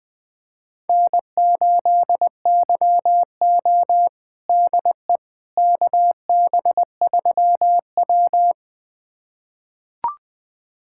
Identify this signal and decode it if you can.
Morse code, heard as a keyed tone, about 20 wpm, 700 Hz. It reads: N8YO DE KB3W
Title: USA Callsign DE Callsign - 4 Characters - 20wpm